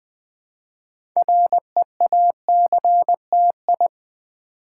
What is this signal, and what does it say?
Morse code, heard as a keyed tone, about 20 wpm, 700 Hz. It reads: REACTI
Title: Build A Word - 3 Letters At A Time - 8-Letter Words - 20wpm